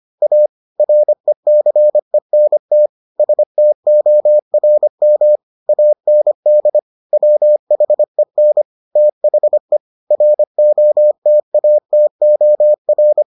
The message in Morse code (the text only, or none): A RECENT STORM AND WHEN THE ROTATOR